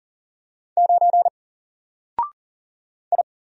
Morse code, heard as a keyed tone, about 40 wpm, 700 Hz. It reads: 9 I
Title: Single Letter-Number Alt 40wpm